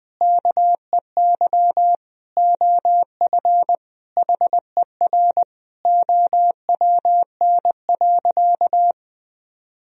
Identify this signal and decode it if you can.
Morse code, heard as a keyed tone, about 21 wpm, 700 Hz. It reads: KEY OF HER OWN.
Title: Sentences from Top 1000 Words - Review Part 4 - 20wpm